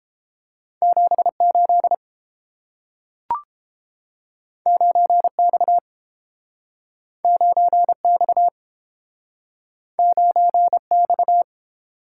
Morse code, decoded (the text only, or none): 78 9X 9X 9X